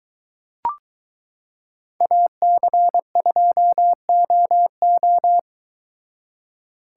Morse code, heard as a keyed tone, about 22 wpm, 700 Hz. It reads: AC2OO